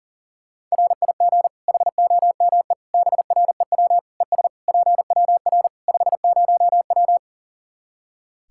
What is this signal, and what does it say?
Morse code, heard as a keyed tone, about 40 wpm, 700 Hz. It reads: RIG HOME BREW ES PWR 50W
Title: Practice QSO - Exchange 2 - Info - 40wpm